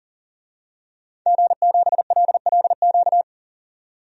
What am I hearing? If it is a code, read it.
Morse code, heard as a keyed tone, about 40 wpm, 700 Hz. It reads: G7LLQ
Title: DXCC-100 Call Signs - 5 Characters - 40wpm